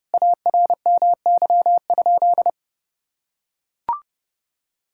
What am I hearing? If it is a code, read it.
Morse code, heard as a keyed tone, about 32 wpm, 700 Hz. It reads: ARMY?
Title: Sentences from Top 1000 Words - Review Part 5 - 30wpm